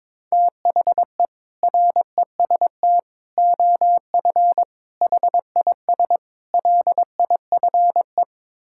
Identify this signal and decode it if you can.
Morse code, heard as a keyed tone, about 22 wpm, 700 Hz. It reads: THE REST OF HIS LIFE
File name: practice / 22wpm / Build A Phrase - 5-Word Phrases - 22wpm